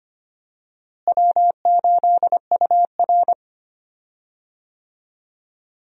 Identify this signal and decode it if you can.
Morse code, heard as a keyed tone, about 25 wpm, 700 Hz. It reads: W8UR
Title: USA Call Signs - 4 Characters - Repeated 2x - 25wpm